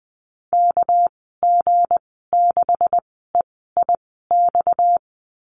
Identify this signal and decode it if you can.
Morse code, heard as a keyed tone, about 20 wpm, spaced as for 14 wpm, 700 Hz. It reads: KG6EIX
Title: 238 Lesson 38g - Letter q - Call Signs 20wpm